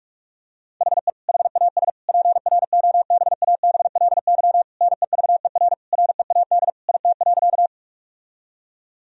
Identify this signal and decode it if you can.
Morse code, heard as a keyed tone, about 42 wpm, 700 Hz. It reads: HE HAS PROBABLY NEVER READ IT.
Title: Sentences from Top 1000 Words - Review Part 4 - 45wpm